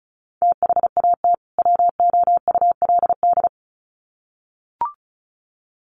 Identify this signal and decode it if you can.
Morse code, heard as a keyed tone, about 34 wpm, 700 Hz. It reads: THAT WOULD
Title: Sets of 2 Words 35wpm